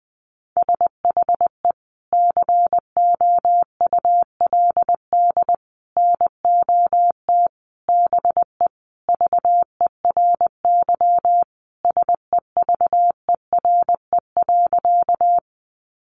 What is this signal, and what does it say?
Morse code, heard as a keyed tone, about 20 wpm, 700 Hz. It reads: SHE COULD NOT BE VERY SEVERE.
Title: Sentences from Top 1400 Words 20wpm